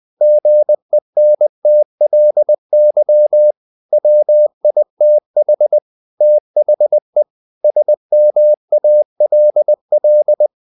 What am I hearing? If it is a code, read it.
Morse code, heard as a keyed tone, about 20 wpm, 600 Hz. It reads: GENTLY WITH THE SMALL
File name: the old man and the sea / the old man and the sea part 1 20wpm